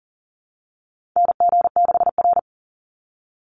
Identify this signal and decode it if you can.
Morse code, heard as a keyed tone, about 40 wpm, 700 Hz. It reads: NG6R
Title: NA SOTA Chaser Callsigns 40wpm